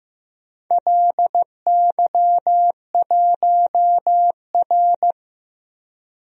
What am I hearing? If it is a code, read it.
Morse code, heard as a keyed tone, about 15 wpm, 700 Hz. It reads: LY1R
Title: DXCC-100 Call Signs - 4 Characters - Repeated 2x - 15wpm